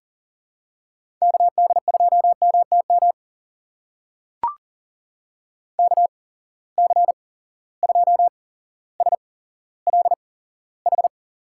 Morse code, heard as a keyed tone, about 40 wpm, spaced as for 17 wpm, 700 Hz. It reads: KD2MTM KC2SLH
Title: USA Call Signs - 6 Characters - 8x Character Spacing - 40wpm